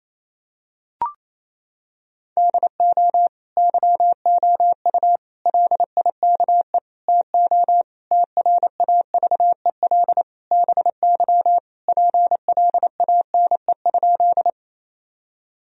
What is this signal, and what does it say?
Morse code, heard as a keyed tone, about 27 wpm, 700 Hz. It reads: DO YOU LIKE TO TRAVEL BY PLANE?
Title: Sentences from Top 1000 Words - Review Part 3 - 28wpm